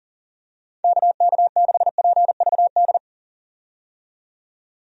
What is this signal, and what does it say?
Morse code, heard as a keyed tone, about 40 wpm, 700 Hz. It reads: KK6PVD
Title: USA Call Signs - 6 Characters - 40wpm